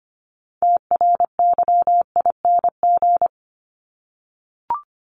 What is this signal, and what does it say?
Morse code, heard as a keyed tone, about 25 wpm, 700 Hz. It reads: TRYING
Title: ING Suffix - Top 100 Words with Base Words 25wpm